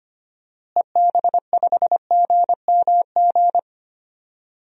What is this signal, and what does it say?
Morse code, heard as a keyed tone, about 25 wpm, 700 Hz. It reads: EB5GMG
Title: DXCC-100 Call Signs - 6 Characters - Repeated 3x - 25wpm